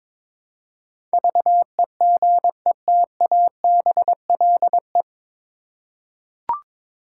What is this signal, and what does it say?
Morse code, heard as a keyed tone, about 22 wpm, 700 Hz. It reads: VEGETABLE
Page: Morse Code Ninja practice 22wpm